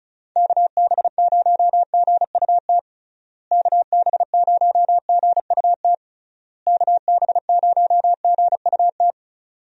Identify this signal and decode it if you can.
Morse code, heard as a keyed tone, about 36 wpm, 700 Hz. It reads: KB0GUT KB0GUT KB0GUT
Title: USA Call Signs - 6 Characters - Repeated 3x - 35wpm